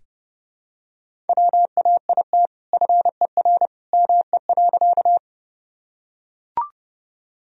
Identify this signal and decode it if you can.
Morse code, heard as a keyed tone, about 30 wpm, 700 Hz. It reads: WAIT FER ME.
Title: Sentences from Top 500 Words and Abbreviations 30wpm